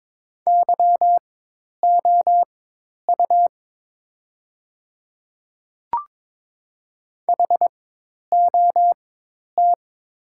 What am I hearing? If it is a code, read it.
Morse code, heard as a keyed tone, about 22 wpm, spaced as for 10 wpm, 700 Hz. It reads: YOU HOT
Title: 4x Character Spacing - Top 100 Words - 22wpm